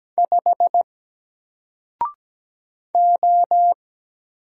Morse code, heard as a keyed tone, about 17 wpm, 700 Hz. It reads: 5 O
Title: Single Letter-Number 17wpm